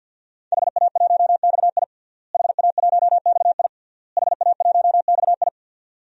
Morse code, heard as a keyed tone, about 49 wpm, 700 Hz. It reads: HA1XI HA1XI HA1XI
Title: DXCC-100 Call Signs - 5 Characters - Repeated 3x - 50wpm